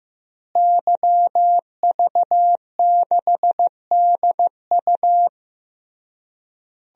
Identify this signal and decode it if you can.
Morse code, heard as a keyed tone, about 16 wpm, 700 Hz. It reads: YV6DU
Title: DXCC-100 Call Signs - 5 Characters - Repeated 3x - 15wpm